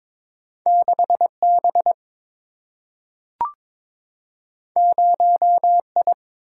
Morse code, heard as a keyed tone, about 21 wpm, 700 Hz. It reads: 6 B 0 I
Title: Single-Letter Single-Number 22wpm